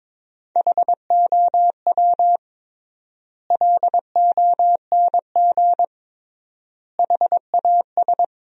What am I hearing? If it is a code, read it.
Morse code, heard as a keyed tone, about 22 wpm, 700 Hz. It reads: HOW LONG HAS
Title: 3x Word-Spacing - Sentences from Top 100 Words - 22wpm